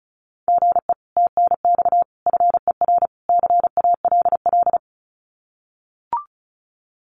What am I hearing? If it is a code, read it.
Morse code, heard as a keyed tone, about 34 wpm, 700 Hz. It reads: GE TNX FER CALL
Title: Common QSO Phrases Encore 35wpm